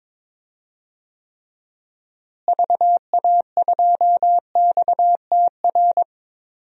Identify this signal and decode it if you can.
Morse code, heard as a keyed tone, about 21 wpm, 700 Hz. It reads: VA2XTR
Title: DXCC-100 Call Signs - 6 Characters - Repeated 3x - 22wpm